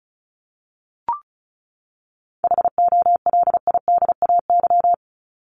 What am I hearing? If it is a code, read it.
Morse code, heard as a keyed tone, about 34 wpm, 700 Hz. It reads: HOLIDAY
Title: Top 1501-1750 Words 35wpm